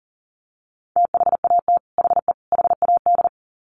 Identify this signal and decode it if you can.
Morse code, heard as a keyed tone, about 40 wpm, 700 Hz. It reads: THAT HE HAD
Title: Sets of 3 Words 40wpm